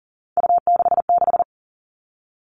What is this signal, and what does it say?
Morse code, heard as a keyed tone, about 40 wpm, 700 Hz. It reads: U66
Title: Three Challenging Letter-Number Combinations 40wpm